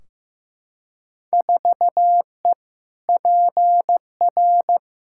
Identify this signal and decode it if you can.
Morse code, heard as a keyed tone, about 15 wpm, 700 Hz. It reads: 4E PR
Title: Field Day Exchanges Encore - 15wpm